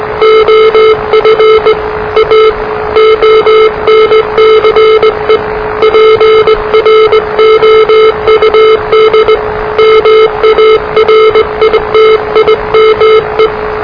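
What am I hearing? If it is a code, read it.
Morse code, heard as a keyed tone, about 20 wpm, 440 Hz, with noise. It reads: OF A ONCE PROUD MARITIME
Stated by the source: Morse challenge from NZ Net, the national amateur radio net of Aotearoa/New Zealand, which meets on 3535.0 kHz.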